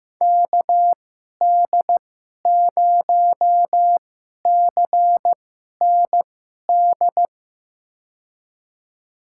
Morse code, heard as a keyed tone, about 16 wpm, spaced as for 11 wpm, 700 Hz. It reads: KD0CND
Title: USA Call Signs - 6 Characters - 2x Character Spacing - 15wpm